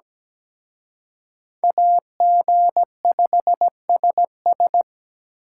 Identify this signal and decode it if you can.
Morse code, heard as a keyed tone, about 20 wpm, 700 Hz. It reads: AG5SS
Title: Speed Racing - USA 2x2 Call Signs 17wpm